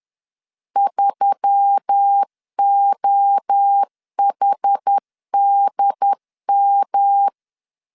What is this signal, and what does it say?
Morse code, heard as a keyed tone, about 10 wpm, 800 Hz. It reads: 3OHDM